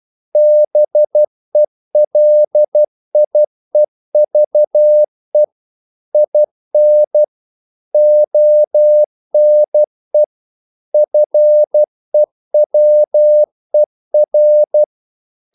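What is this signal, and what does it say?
Morse code, heard as a keyed tone, about 12 wpm, 600 Hz. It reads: BELIEVE IN ONE FEWER
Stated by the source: Morse practice sending English quotations.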